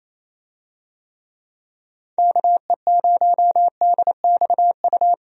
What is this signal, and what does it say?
Morse code, heard as a keyed tone, about 28 wpm, 700 Hz. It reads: KE0DXU
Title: USA Call Signs - 6 Characters - 28wpm